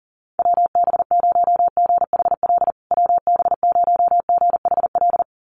Morse code, heard as a keyed tone, about 40 wpm, 700 Hz. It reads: WB0GHL WB0GHL
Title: USA Call Signs Repeated 2x 40wpm